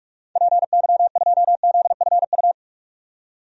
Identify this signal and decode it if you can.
Morse code, heard as a keyed tone, about 46 wpm, 700 Hz. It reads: PY2ZFU
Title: DXCC-100 Call Signs - 6 Characters - 45wpm